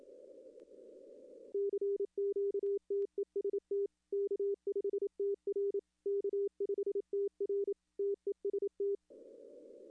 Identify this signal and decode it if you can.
Morse code, heard as a keyed tone, about 28 wpm, 400 Hz. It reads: CQTEST K5TR K5TR TEST
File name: contests / arrl-160-k5tr-op-w5tn -2300